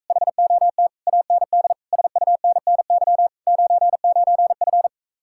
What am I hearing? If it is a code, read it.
Morse code, heard as a keyed tone, about 45 wpm, 700 Hz. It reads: HOT AND SUNNY 99F